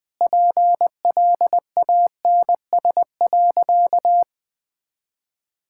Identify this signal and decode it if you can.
Morse code, heard as a keyed tone, about 20 wpm, 700 Hz. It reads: PLANS.
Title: Sentences with 5-Word Phrases - Spoken Beginning - 20wpm